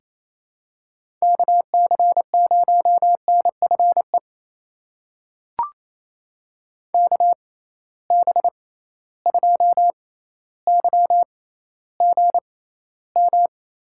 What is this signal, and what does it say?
Morse code, heard as a keyed tone, about 28 wpm, spaced as for 14 wpm, 700 Hz. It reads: KC0NFE KB2YGM
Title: USA Call Signs - 6 Characters - 6x Character Spacing - 28wpm